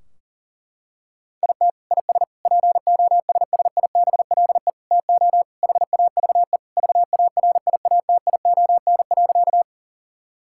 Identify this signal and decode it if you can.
Morse code, heard as a keyed tone, about 40 wpm, 700 Hz. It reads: IT IS POSSIBLE TO HAVE VARIATION.
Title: Sentences from Top 1600 Words 40wpm